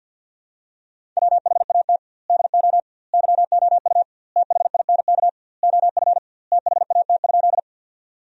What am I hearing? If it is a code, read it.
Morse code, heard as a keyed tone, about 50 wpm, 700 Hz. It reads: WHAT DO YOU THINK OF THAT?
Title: 2x Word-Spacing - Sentences from Top 200 Words - 50wpm